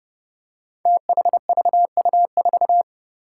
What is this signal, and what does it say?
Morse code, heard as a keyed tone, about 30 wpm, 700 Hz. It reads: THVU4
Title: Five Challenging Letter-Number Combinations 30wpm